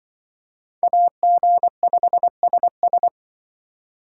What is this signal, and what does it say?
Morse code, heard as a keyed tone, about 26 wpm, 700 Hz. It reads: AG5SS